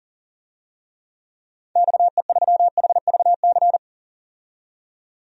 Morse code, heard as a keyed tone, about 40 wpm, 700 Hz. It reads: XE3HVC
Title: DXCC-100 Call Signs - 6 Characters - 40wpm